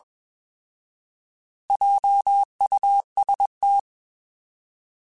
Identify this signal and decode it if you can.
Morse code, heard as a keyed tone, about 21 wpm, 800 Hz. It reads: JUST
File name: Alphabetical / Alphabetical mp3s / Js @20